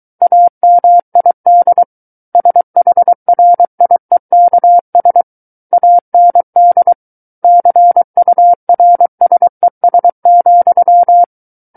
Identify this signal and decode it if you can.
Morse code, heard as a keyed tone, about 23 wpm, 700 Hz. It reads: AMID SHRIEKS AND CURSES,
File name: Book One / 14-Chapter 14 - 23 WPM